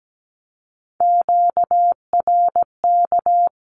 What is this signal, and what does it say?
Morse code, heard as a keyed tone, about 17 wpm, 700 Hz. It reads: QRK
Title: Q-Codes 17wpm